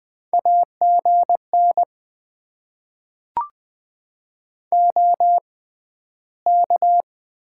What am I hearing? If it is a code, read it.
Morse code, heard as a keyed tone, about 20 wpm, spaced as for 12 wpm, 700 Hz. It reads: AGN OK
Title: Top 90 QSO Elements - 6x Character Spacing - 20wpm